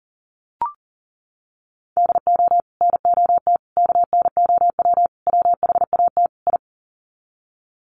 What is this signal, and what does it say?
Morse code, heard as a keyed tone, about 40 wpm, 700 Hz. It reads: DO NOT KNOW WHAT I
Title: Sets of 5 Words 40wpm